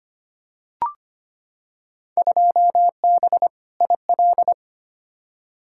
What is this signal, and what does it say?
Morse code, heard as a keyed tone, about 25 wpm, 700 Hz. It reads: 2B IL
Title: Field Day Exchanges Encore - 25wpm